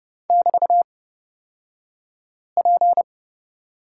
Morse code, heard as a keyed tone, about 30 wpm, spaced as for 5 wpm, 700 Hz. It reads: <BT>P
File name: Lesson 36d - BT - Two Characters 30wpm